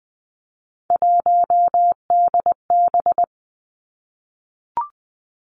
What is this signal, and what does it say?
Morse code, heard as a keyed tone, about 19 wpm, 700 Hz. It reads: 1DB
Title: Three Challenging Letter-Number Combinations 20wpm